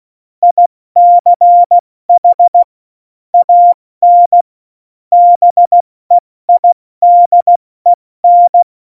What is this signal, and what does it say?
Morse code, heard as a keyed tone, about 16 wpm, 700 Hz. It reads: ICH AN BEIDEN
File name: Morsekurs / Witze / Witz 20. November 2018, 70 F80